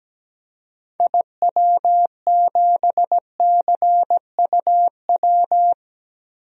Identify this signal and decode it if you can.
Morse code, heard as a keyed tone, about 17 wpm, 700 Hz. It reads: IW7CUW
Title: DXCC-100 Call Signs - 6 Characters - 17wpm